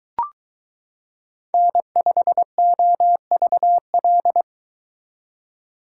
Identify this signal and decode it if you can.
Morse code, heard as a keyed tone, about 22 wpm, 700 Hz. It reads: N5OVL